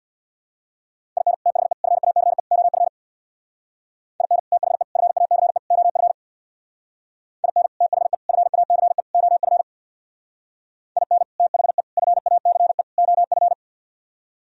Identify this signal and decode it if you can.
Morse code, heard as a keyed tone, about 64 wpm, 700 Hz. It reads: IN THE FACE OF IN THE FACE OF IN THE FACE OF IN THE FACE OF